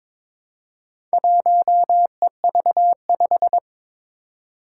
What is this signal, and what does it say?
Morse code, heard as a keyed tone, about 22 wpm, 700 Hz. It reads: 1EV5
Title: Four Challenging Letter-Number Combinations 22wpm